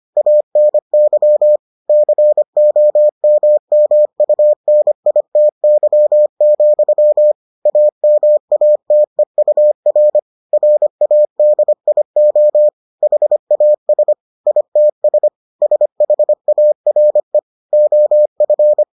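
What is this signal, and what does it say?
Morse code, heard as a keyed tone, about 25 wpm, 600 Hz. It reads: ANY COMMUNITY, AMATEUR RADIO HAS ITS SHARE OF
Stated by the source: Morse edition of the Foundations of Amateur Radio episodes.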